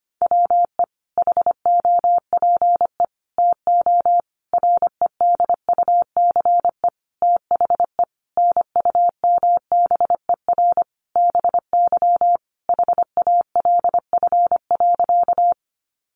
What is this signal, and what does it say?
Morse code, heard as a keyed tone, about 25 wpm, 700 Hz. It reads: WE HOPE TO REDUCE THE NUMBER BY HALF.
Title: Sentences from Top 1100 Words 25wpm